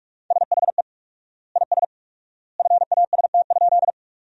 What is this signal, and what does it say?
Morse code, heard as a keyed tone, about 46 wpm, 700 Hz. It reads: SHE IS FAST?
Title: 4x Word-Spacing - Sentences from Top 400 Words - 45wpm